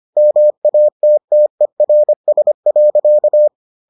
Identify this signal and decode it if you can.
Morse code, heard as a keyed tone, about 25 wpm, 600 Hz. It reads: MATTERS.